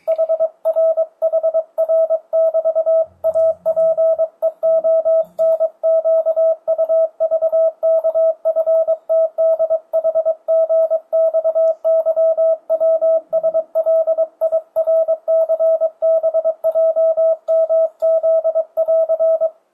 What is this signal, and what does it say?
Morse code, heard as a keyed tone, about 22 wpm, 650 Hz, with radio noise. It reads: HRHR<BT>APEONQUVKFTDHGXYWSLIRCBJMZ<AR>
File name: 2025cw / speedup / 23wpm